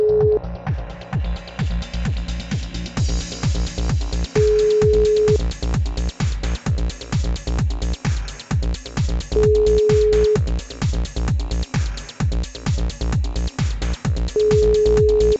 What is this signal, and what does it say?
Ringback tone (Europe, 425 Hz).